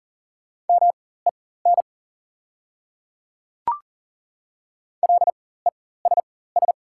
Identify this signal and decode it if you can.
Morse code, heard as a keyed tone, about 40 wpm, spaced as for 19 wpm, 700 Hz. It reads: MEN LESS
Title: 4x Character Spacing - Top 500 Words - 40wpm